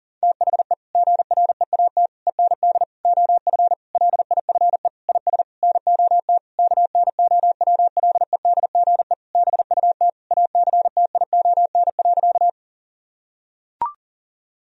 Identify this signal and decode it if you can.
Morse code, heard as a keyed tone, about 40 wpm, 700 Hz. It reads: THE GREAT END OF LIFE IS NOT KNOWLEDGE BUT ACTION.
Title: Sentences from Top 1000 Words - Review Part 4 - 40wpm